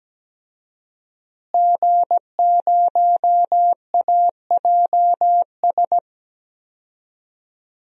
Morse code, heard as a keyed tone, about 17 wpm, 700 Hz. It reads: G0AJS
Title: DXCC-100 Call Signs - 5 Characters - Repeated 3x - 17wpm